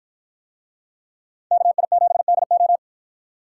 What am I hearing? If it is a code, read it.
Morse code, heard as a keyed tone, about 50 wpm, 700 Hz. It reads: KI7DO